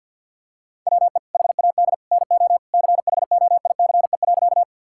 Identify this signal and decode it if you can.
Morse code, heard as a keyed tone, about 46 wpm, 700 Hz. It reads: WE HAD NO CHOICE.
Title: Sentences from Top 1000 Words - Review Part 6 - 50wpm